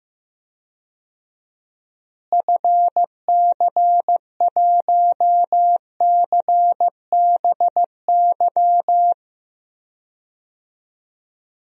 Morse code, heard as a keyed tone, about 15 wpm, 700 Hz. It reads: FC1CBY
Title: DXCC-100 Call Signs - 6 Characters - Repeated 2x - 15wpm